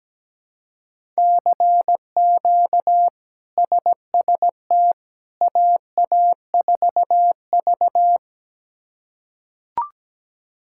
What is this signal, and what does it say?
Morse code, heard as a keyed tone, about 17 wpm, 700 Hz. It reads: CQ SST AA4V
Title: K1USN SST QSO Practice - 17wpm